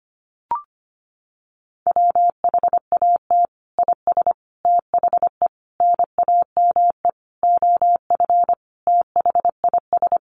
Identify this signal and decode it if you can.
Morse code, heard as a keyed tone, about 25 wpm, 700 Hz. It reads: WHAT IS THE NAME OF THIS
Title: Sentences from Top 1500 Words 25wpm